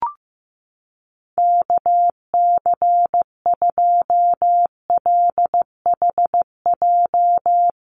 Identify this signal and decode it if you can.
Morse code, heard as a keyed tone, about 15 wpm, 700 Hz. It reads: KC2LHJ
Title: USA Call Signs Repeated 2x 15wpm